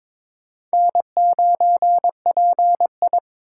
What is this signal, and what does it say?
Morse code, heard as a keyed tone, about 23 wpm, 700 Hz. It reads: N9PI
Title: USA Call Signs - 2x Character Spacing - 22wpm